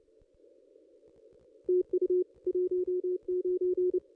Morse code, heard as a keyed tone, about 28 wpm, 360 Hz. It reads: TU 19
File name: contests / ss-cw-k5tr-op-n5rz -1900